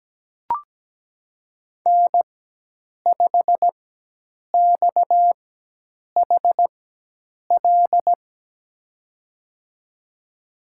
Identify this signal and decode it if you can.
Morse code, heard as a keyed tone, about 17 wpm, spaced as for 8 wpm, 700 Hz. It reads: N5XHL
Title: USA Call Signs - 5 Characters - 4x Character Spacing - 17wpm